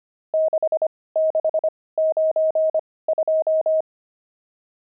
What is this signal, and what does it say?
Morse code, heard as a keyed tone, about 25 wpm, spaced as for 18 wpm, 630 Hz. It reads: 6692
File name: pangram-fs-18-cs-25-nums7